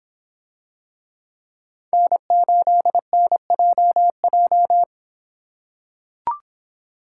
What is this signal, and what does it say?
Morse code, heard as a keyed tone, about 24 wpm, 700 Hz. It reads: N8NJJ